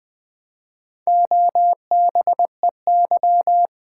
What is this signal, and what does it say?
Morse code, heard as a keyed tone, about 20 wpm, 700 Hz. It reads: OBEY
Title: Build A Word - 1 Letter At A Time - 4-Letter Words - 20wpm